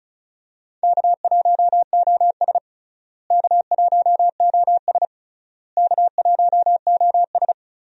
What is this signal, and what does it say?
Morse code, heard as a keyed tone, about 36 wpm, 700 Hz. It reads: K1OS K1OS K1OS
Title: USA Call Signs - 4 Characters - Repeated 3x - 35wpm